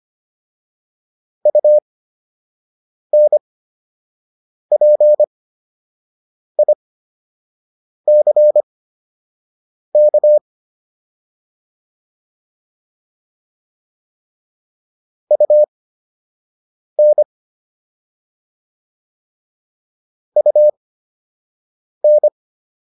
Morse code, heard as a keyed tone, about 25 wpm, spaced as for 6 wpm, 600 Hz. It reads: UNPICK UN UN